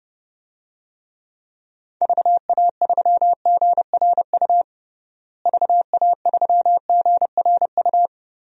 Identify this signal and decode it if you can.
Morse code, heard as a keyed tone, about 30 wpm, 700 Hz. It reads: VA3GRU VA3GRU
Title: DXCC-100 Call Signs - 6 Characters - Repeated 2x - 30wpm